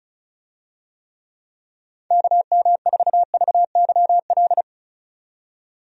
Morse code, heard as a keyed tone, about 36 wpm, 700 Hz. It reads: KM4VYL
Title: USA Call Signs - 6 Characters - 35wpm